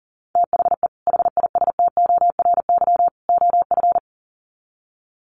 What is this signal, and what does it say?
Morse code, heard as a keyed tone, about 40 wpm, 700 Hz. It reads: THE HISTORY OF
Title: Sets of 3 Words 40wpm